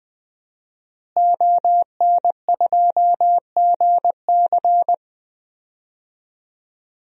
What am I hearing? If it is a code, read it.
Morse code, heard as a keyed tone, about 21 wpm, 700 Hz. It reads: ON2GC
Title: DXCC-100 Call Signs - 5 Characters - Repeated 3x - 20wpm